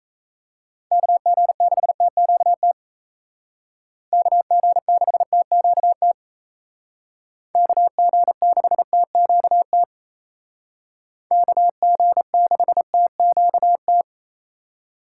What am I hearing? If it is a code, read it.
Morse code, heard as a keyed tone, about 33 wpm, 700 Hz. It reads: KG6TQT KG6TQT KG6TQT KG6TQT
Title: Speed Racing - USA 2x3 Call Signs 28wpm